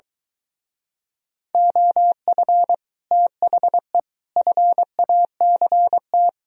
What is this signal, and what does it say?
Morse code, heard as a keyed tone, about 23 wpm, 700 Hz. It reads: OF THE FACT